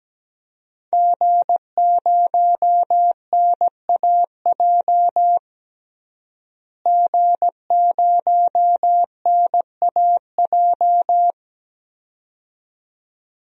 Morse code, heard as a keyed tone, about 17 wpm, 700 Hz. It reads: G0NAJ G0NAJ
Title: DXCC-100 Call Signs - 5 Characters - Repeated 2x - 17wpm